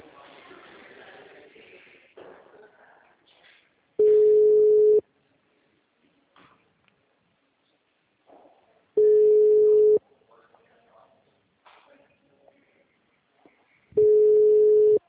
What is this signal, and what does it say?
Ringback tone (Europe, 425 Hz).